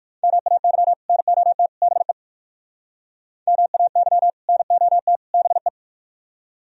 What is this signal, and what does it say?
Morse code, heard as a keyed tone, about 59 wpm, 700 Hz. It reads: MAY NOT BE MAY NOT BE